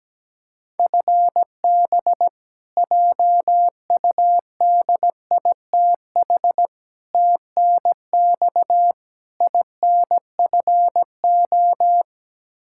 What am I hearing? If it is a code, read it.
Morse code, heard as a keyed tone, about 17 wpm, 700 Hz. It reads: FB JUDITH TNX INFO
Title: K1USN SST QSO Practice 17wpm